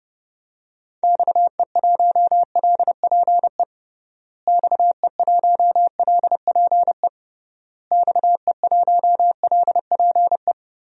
Morse code, heard as a keyed tone, about 30 wpm, 700 Hz. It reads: XE1LPE XE1LPE XE1LPE
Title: DXCC-100 Call Signs - 6 Characters - Repeated 3x - 30wpm